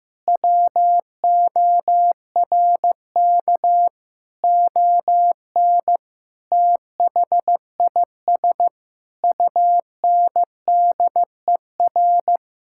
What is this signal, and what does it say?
Morse code, heard as a keyed tone, about 16 wpm, 700 Hz. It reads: WORK ON THIS UNDER
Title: Sentences from Top 1000 Words - Review Part 6 - 15wpm